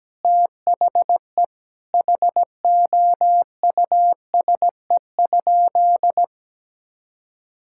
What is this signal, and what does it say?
Morse code, heard as a keyed tone, about 17 wpm, 700 Hz. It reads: THE HOUSE?
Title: Sentences from Top 1000 Words - Review Part 5 - 17wpm